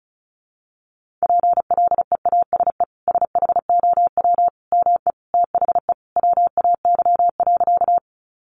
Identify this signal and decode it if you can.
Morse code, heard as a keyed tone, about 34 wpm, 700 Hz. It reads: PLEASE SHOW ME THE WAY.
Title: Sentences from Top 1500 Words 35wpm